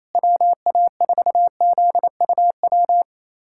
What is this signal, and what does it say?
Morse code, heard as a keyed tone, about 27 wpm, 700 Hz. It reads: WA4ZUW
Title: USA Callsign DE Callsign 28wpm